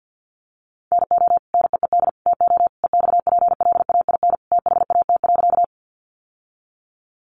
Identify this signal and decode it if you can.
Morse code, heard as a keyed tone, about 50 wpm, 700 Hz. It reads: NO NEED TO EXPLAIN THAT.
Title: Sentences from Top 1100 Words 50wpm